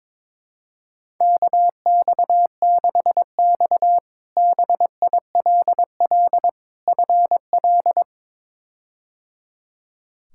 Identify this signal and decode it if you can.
Morse code, heard as a keyed tone, about 22 wpm, 700 Hz. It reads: KX6X BILL FL
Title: K1USN SST QSO Practice - 22wpm